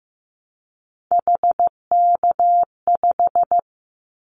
Morse code, heard as a keyed tone, about 15 wpm, 700 Hz. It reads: HK5
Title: Callsign Prefixes for Top 100 DXCC - 15wpm